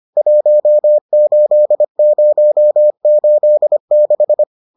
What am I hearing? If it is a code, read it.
Morse code, heard as a keyed tone, about 25 wpm, 600 Hz. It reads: 18086